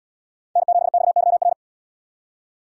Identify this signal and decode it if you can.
Morse code, heard as a keyed tone, about 72 wpm, 700 Hz. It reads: N7XYR